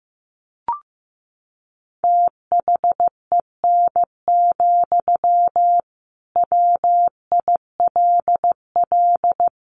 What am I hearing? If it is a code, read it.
Morse code, heard as a keyed tone, about 15 wpm, 700 Hz. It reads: THEN, WILL
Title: Sentences from Top 1400 Words 15wpm